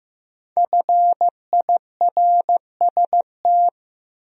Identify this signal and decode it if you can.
Morse code, heard as a keyed tone, about 18 wpm, 700 Hz. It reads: FIRST
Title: Speed-Racing Top 100 Words 15wpm